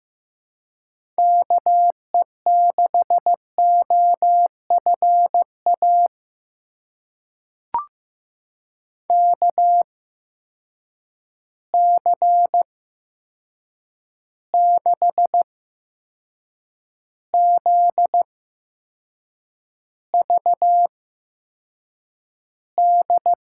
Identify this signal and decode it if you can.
Morse code, heard as a keyed tone, about 15 wpm, spaced as for 6 wpm, 700 Hz. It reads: KE6OFA KC6ZVD
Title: USA Call Signs - 6 Characters - 8x Character Spacing - 15wpm